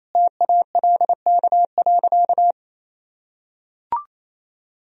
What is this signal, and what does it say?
Morse code, heard as a keyed tone, about 28 wpm, 700 Hz. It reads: TALK.
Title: Sentences from Top 400 Words and Abbreviations 28wpm